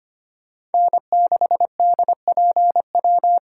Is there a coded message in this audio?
Yes: Morse code from 0.7 to 3.4 s, heard as a keyed tone, about 33 wpm, 700 Hz. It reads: N6DPW